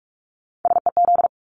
Morse code, heard as a keyed tone, about 44 wpm, 700 Hz. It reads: SEZ
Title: Abbreviations 45wpm